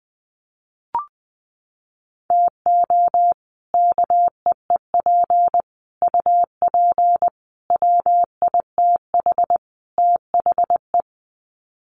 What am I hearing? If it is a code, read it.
Morse code, heard as a keyed tone, about 20 wpm, 700 Hz. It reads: TO KEEP UP WITH THE
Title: Sets of 5 Words Encore 20wpm